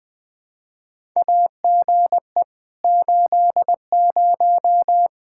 Morse code, heard as a keyed tone, about 20 wpm, 700 Hz. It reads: AGE 80
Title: K1USN SST QSO Practice 20wpm